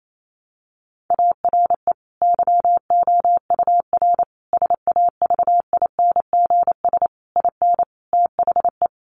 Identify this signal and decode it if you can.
Morse code, heard as a keyed tone, about 28 wpm, 700 Hz. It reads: ARE YOUR SAVINGS IN THE
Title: Sentences from Top 1200 Words 28wpm